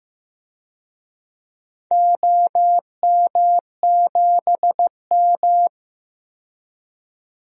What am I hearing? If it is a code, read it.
Morse code, heard as a keyed tone, about 15 wpm, 700 Hz. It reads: OM7M
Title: DXCC-100 Call Signs - 4 Characters - 15wpm